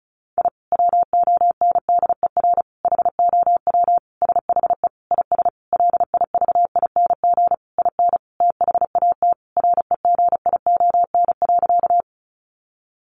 Morse code, heard as a keyed tone, about 34 wpm, 700 Hz. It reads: I WONDER HOW SHE IS LIVING IN THAT REGION.
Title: Sentences from Top 1500 Words 35wpm